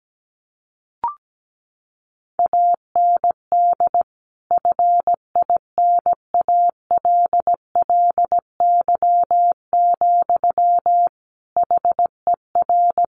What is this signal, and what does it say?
Morse code, heard as a keyed tone, about 17 wpm, 700 Hz. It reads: AND FINALLY, HER
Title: Sentences from Top 1100 Words 17wpm